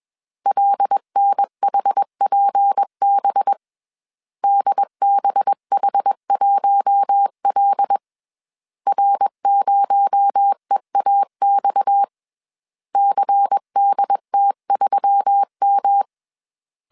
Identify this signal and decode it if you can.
Morse code, heard as a keyed tone, about 21 wpm, 800 Hz. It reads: LNHPB DBH1L R0EAX CDT3M
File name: CW / DET / 1 6 1500C S A 20wpm 2048